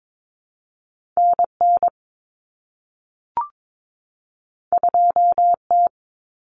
Morse code, heard as a keyed tone, about 22 wpm, 700 Hz. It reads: NN 2T
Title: Common Cut Numbers - 2-Digit 22wpm